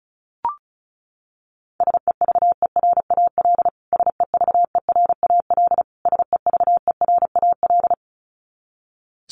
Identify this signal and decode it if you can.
Morse code, heard as a keyed tone, about 34 wpm, 700 Hz. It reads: SEVERAL SEVERAL SEVERAL
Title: Top 500 Words Repeated 35wpm